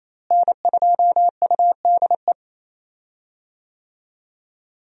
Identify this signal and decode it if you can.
Morse code, heard as a keyed tone, about 27 wpm, 700 Hz. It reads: N2UDE
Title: USA Call Signs - 5 Characters - Repeated 3x - 28wpm